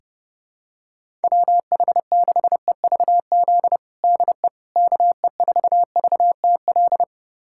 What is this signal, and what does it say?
Morse code, heard as a keyed tone, about 30 wpm, 700 Hz. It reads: WH6EVZ DE KE4VTL
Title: USA Callsign DE Callsign - 6 Characters - 30wpm